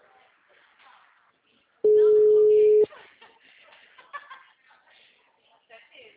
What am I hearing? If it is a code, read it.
Ringback tone (Europe, 425 Hz), with speech.
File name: out- -2020- - - .38320